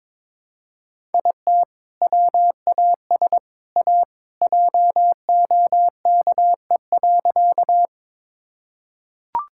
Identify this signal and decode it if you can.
Morse code, heard as a keyed tone, about 22 wpm, 700 Hz. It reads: IT WAS A JOKE.
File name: practice / 22wpm / Sentences with 3-Word Phrases - 22wpm